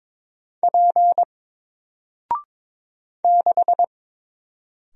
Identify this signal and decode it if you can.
Morse code, heard as a keyed tone, about 22 wpm, 700 Hz. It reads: P 6
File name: Single-Letter-Number-22wpm